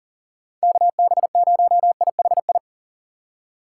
Morse code, heard as a keyed tone, about 40 wpm, 700 Hz. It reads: KB0IHI
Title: USA Call Signs - 2x Character Spacing - 40wpm